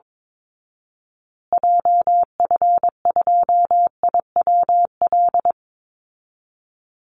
Morse code, heard as a keyed tone, about 22 wpm, 700 Hz. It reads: JF2IWL
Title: Call Signs Repeated 2x - 22wpm